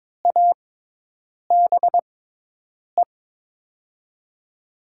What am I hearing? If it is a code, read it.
Morse code, heard as a keyed tone, about 22 wpm, spaced as for 8 wpm, 700 Hz. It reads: ABE
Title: USA Call Signs - 5 Characters - 6x Character Spacing - 22wpm